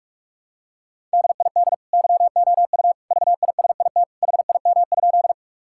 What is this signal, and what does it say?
Morse code, heard as a keyed tone, about 46 wpm, 700 Hz. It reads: DID YOU VISIT HIM?
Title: Sentences from Top 1000 Words - Review Part 6 - 45wpm